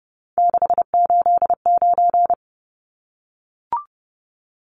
Morse code, heard as a keyed tone, about 30 wpm, 700 Hz. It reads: 689
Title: Numbers - 3-Digit 30wpm